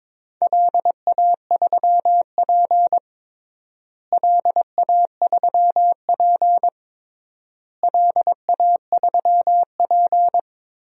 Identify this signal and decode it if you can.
Morse code, heard as a keyed tone, about 21 wpm, 700 Hz. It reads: LA3P LA3P LA3P
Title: DXCC-100 Call Signs - 4 Characters - Repeated 3x - 22wpm